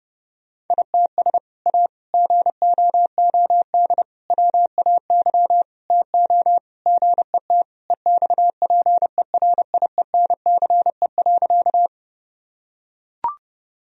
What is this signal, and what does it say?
Morse code, heard as a keyed tone, about 32 wpm, 700 Hz. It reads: ITS A GOOD WAY TO GET EXPERIENCE.
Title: Sentences from Top 1000 Words - Review Part 3 - 30wpm